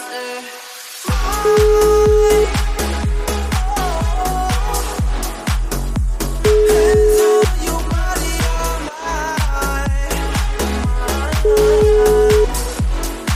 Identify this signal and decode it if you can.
Ringback tone (Europe, 425 Hz), with noise.